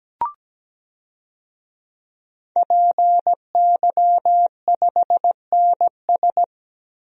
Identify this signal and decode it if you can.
Morse code, heard as a keyed tone, about 17 wpm, 700 Hz. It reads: PY5NS
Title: DXCC-100 Call Signs - 5 Characters - Repeated 3x - 17wpm